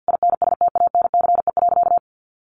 Morse code, heard as a keyed tone, about 50 wpm, 700 Hz. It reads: INSTANCE.
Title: Sentences from Top 800 Words 50wpm